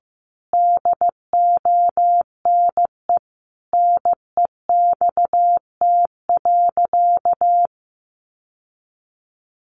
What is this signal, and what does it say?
Morse code, heard as a keyed tone, about 15 wpm, 700 Hz. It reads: DONE NEXT.
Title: Sentences from Top 1200 Words 15wpm